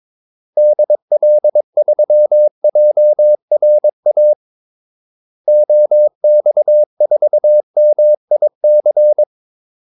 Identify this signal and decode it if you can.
Morse code, heard as a keyed tone, about 22 wpm, 600 Hz. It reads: DL3JRA OX4MIC
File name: Callsigns-22wpm